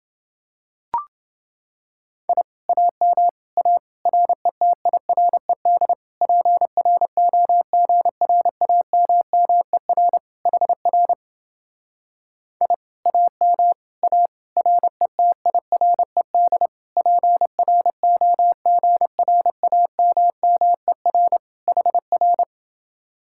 Morse code, heard as a keyed tone, about 30 wpm, 700 Hz. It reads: I AM A RETIRED PROGRAMMER HR I AM A RETIRED PROGRAMMER HR